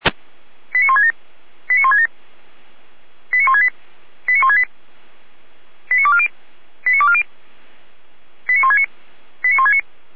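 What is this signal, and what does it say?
Five-tone selective call (ZVEI), heard as quick runs of five tones, tones of about 70 ms. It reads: 89167 89167 89168 89168 89240 89240 89169 89169
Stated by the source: ZVEI five-tone call received by SDR.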